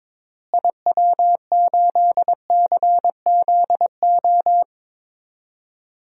Morse code, heard as a keyed tone, about 21 wpm, 700 Hz. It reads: IW8CZO
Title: DXCC-100 Call Signs - 6 Characters - Repeated 2x - 22wpm